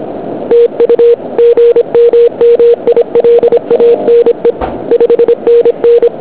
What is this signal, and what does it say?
Morse code, heard as a keyed tone, about 26 wpm, 470 Hz. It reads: TU GM MILANE 5NN
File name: ARCHIV / QRP2008 / OK1UN OK2UQ